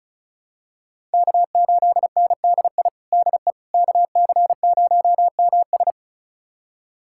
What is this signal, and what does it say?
Morse code, heard as a keyed tone, about 34 wpm, 700 Hz. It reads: K8NDI DE KC0MS
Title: USA Callsign DE Callsign - 5 Characters - 35wpm